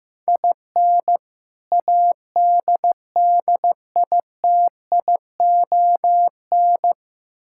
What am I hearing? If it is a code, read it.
Morse code, heard as a keyed tone, about 15 wpm, 700 Hz. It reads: IN ADDITION
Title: Sentences from Top 1000 Words - Review Part 6 - 15wpm